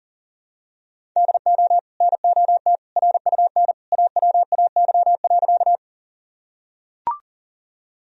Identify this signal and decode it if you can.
Morse code, heard as a keyed tone, about 40 wpm, 700 Hz. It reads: DO NOT RUN AWAY.
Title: Sentences from Top 1000 Words - Review Part 1 - 40wpm